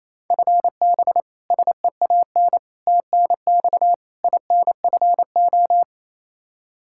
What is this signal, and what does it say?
Morse code, heard as a keyed tone, about 27 wpm, 700 Hz. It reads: FB SEAN TNX INFO
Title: Practice QSO - Exchange 3 - More Info - 28wpm